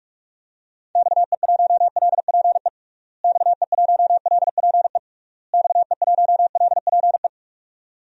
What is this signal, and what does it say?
Morse code, heard as a keyed tone, about 46 wpm, 700 Hz. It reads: XE1LPE XE1LPE XE1LPE
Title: DXCC-100 Call Signs - 6 Characters - Repeated 3x - 45wpm